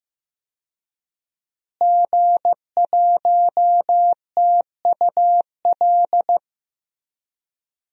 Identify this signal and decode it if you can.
Morse code, heard as a keyed tone, about 15 wpm, 700 Hz. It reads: G1TUL
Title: DXCC-100 Call Signs - 5 Characters - Repeated 3x - 15wpm